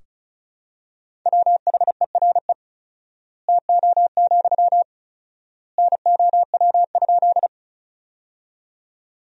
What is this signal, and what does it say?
Morse code, heard as a keyed tone, about 36 wpm, 700 Hz. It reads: WHERE TO, NOW?
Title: 4x Word-Spacing - Sentences from Top 200 Words - 35wpm